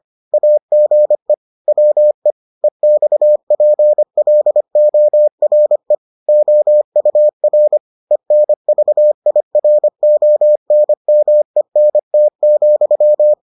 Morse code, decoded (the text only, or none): AGE WE EXPLORE OUR ENVIRONMENT,